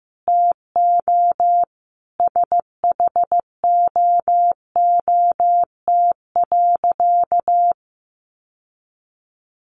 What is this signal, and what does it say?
Morse code, heard as a keyed tone, about 15 wpm, 700 Hz. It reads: TO SHOOT.
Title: Sentences from Top 1200 Words 15wpm